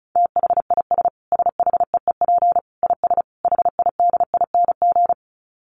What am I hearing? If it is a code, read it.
Morse code, heard as a keyed tone, about 34 wpm, 700 Hz. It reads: THIS SHEEP IS HIDING
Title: Dit Challenge 35wpm